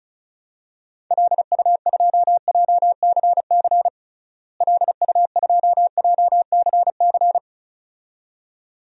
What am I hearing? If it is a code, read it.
Morse code, heard as a keyed tone, about 36 wpm, 700 Hz. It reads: LU2JCC LU2JCC
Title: DXCC-100 Call Signs - 6 Characters - Repeated 2x - 35wpm